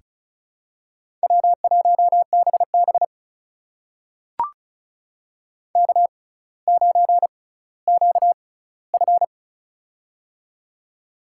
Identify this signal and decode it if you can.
Morse code, heard as a keyed tone, about 36 wpm, spaced as for 18 wpm, 700 Hz. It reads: W1BB K9QF
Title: USA Call Signs - 4 Characters - 6x Character Spacing - 35wpm